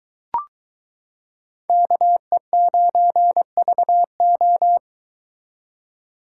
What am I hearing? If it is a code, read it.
Morse code, heard as a keyed tone, about 23 wpm, 700 Hz. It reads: KE9VO